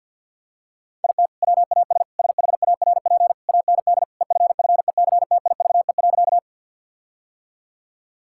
Morse code, heard as a keyed tone, about 45 wpm, 700 Hz. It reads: IT WAS SHARP AND EFFECTIVE.
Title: Sentences from Top 1600 Words 50wpm